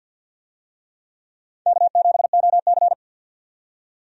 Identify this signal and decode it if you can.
Morse code, heard as a keyed tone, about 46 wpm, 700 Hz. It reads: K7OC
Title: USA Call Signs - 4 Characters - 50wpm